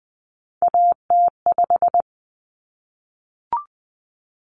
Morse code, heard as a keyed tone, about 20 wpm, 700 Hz. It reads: AT5
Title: Common Cut Numbers - 3-Digit 20wpm